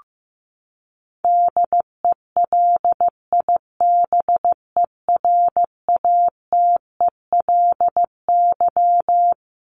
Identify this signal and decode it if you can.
Morse code, heard as a keyed tone, about 15 wpm, 700 Hz. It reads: DELIBERATELY
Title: Top 1751-2000 Words 15wpm